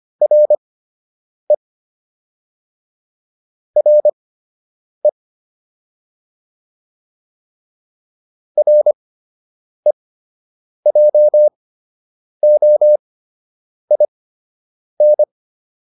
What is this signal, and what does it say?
Morse code, heard as a keyed tone, about 25 wpm, spaced as for 8 wpm, 600 Hz. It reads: RE RE REJOIN